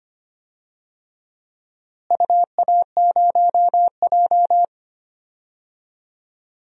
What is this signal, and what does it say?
Morse code, heard as a keyed tone, about 25 wpm, 700 Hz. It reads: UA0J
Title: DXCC-100 Call Signs - 4 Characters - Repeated 3x - 25wpm